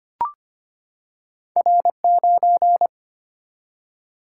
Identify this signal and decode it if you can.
Morse code, heard as a keyed tone, about 25 wpm, 700 Hz. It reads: R 9
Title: Single-Letter Single-Number 25wpm